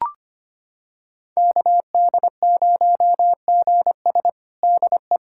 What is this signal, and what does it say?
Morse code, heard as a keyed tone, about 25 wpm, 700 Hz. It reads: KD0GS DE
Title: USA Callsign DE Callsign - 5 Characters - 25wpm